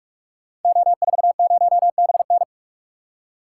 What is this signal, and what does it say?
Morse code, heard as a keyed tone, about 44 wpm, 700 Hz. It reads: O40BN
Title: Code Groups - 5 Letter:Numbers - 45wpm